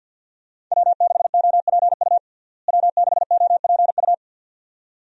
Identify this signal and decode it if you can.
Morse code, heard as a keyed tone, about 52 wpm, 700 Hz. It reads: W6OPU W6OPU
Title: USA Call Signs - 5 Characters - Repeated 2x - 50wpm